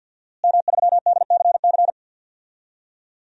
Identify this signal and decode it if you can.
Morse code, heard as a keyed tone, about 52 wpm, 700 Hz. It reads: M3DCC
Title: DXCC-100 Call Signs - 5 Characters - Repeated 3x - 50wpm